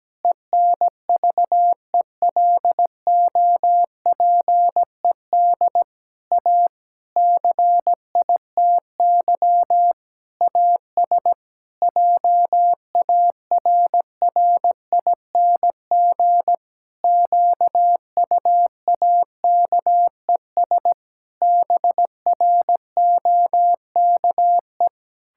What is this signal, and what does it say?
Morse code, heard as a keyed tone, about 17 wpm, 700 Hz. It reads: ENVELOPED A CITY AS JARRING QUAKES BROKE
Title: Pangram Challenge v2 - 17wpm